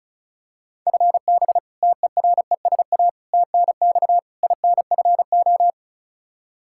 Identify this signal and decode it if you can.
Morse code, heard as a keyed tone, about 36 wpm, 700 Hz. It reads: FB TERESA TNX INFO
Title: Practice QSO - Exchange 3 - More Info - 35wpm